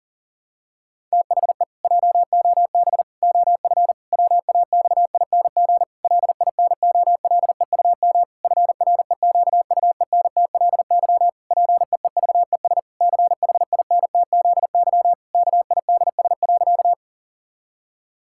Morse code, heard as a keyed tone, about 40 wpm, 700 Hz. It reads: THE JOB OF WAXING LINOLEUM FREQUENTLY PEEVES CHINTZY KIDS.
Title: Pangram Challenge v2 - 40wpm